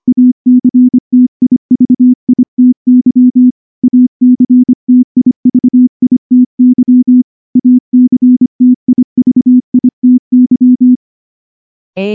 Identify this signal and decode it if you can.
Morse code, heard as a keyed tone, about 24 wpm, 250 Hz. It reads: ACTIVITY ACTIVITY ACTIVITY
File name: morse / morse-training-25WPM